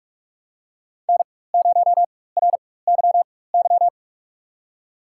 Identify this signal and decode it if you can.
Morse code, heard as a keyed tone, about 46 wpm, spaced as for 21 wpm, 700 Hz. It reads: N0RYY
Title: USA Call Signs - 4x Character Spacing - 45wpm